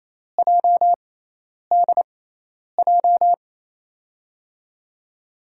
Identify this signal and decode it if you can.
Morse code, heard as a keyed tone, about 28 wpm, spaced as for 10 wpm, 700 Hz. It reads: JDJ
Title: USA Call Signs - 6 Characters - 6x Character Spacing - 28wpm